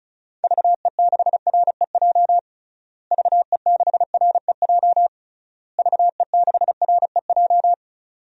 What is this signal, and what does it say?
Morse code, heard as a keyed tone, about 36 wpm, 700 Hz. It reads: VE6REJ VE6REJ VE6REJ
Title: DXCC-100 Call Signs - 6 Characters - Repeated 3x - 35wpm